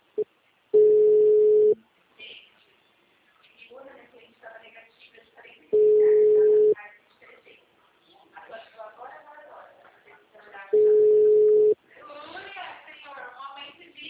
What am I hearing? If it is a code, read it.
Ringback tone (Europe, 425 Hz), with speech.
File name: out- -2005- - - .28826